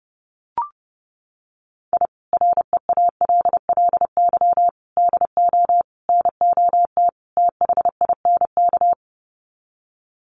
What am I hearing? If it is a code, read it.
Morse code, heard as a keyed tone, about 30 wpm, 700 Hz. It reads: I REALLY DO NOT THINK
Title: Sets of 5 Words Encore2 30wpm